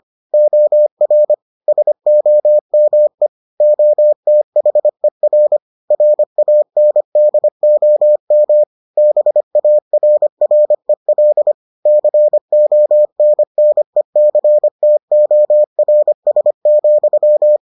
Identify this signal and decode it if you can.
Morse code, heard as a keyed tone, about 25 wpm, 600 Hz. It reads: OR SOME OTHER RANDOM BARREL CONNECTORS,